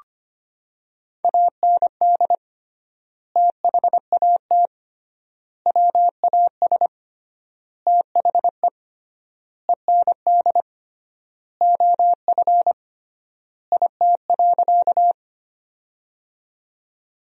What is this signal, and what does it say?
Morse code, heard as a keyed tone, about 25 wpm, 700 Hz. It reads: AND THAT WAS THE END OF IT.
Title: 3x Word-Spacing - Sentences from Top 200 Words - 25wpm